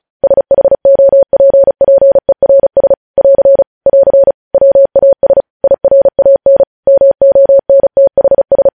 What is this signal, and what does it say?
Morse code, heard as a keyed tone, about 36 wpm, 550 Hz. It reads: SHOPPERS <AR> <AR> WAS IRAN MONTHS
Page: news headlines in morse code at 35 wpm